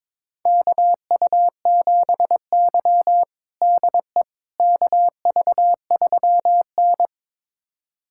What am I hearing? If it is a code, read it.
Morse code, heard as a keyed tone, about 22 wpm, 700 Hz. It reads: KU7Y DE KV3N
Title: USA Callsign DE Callsign - 4 Characters - 22wpm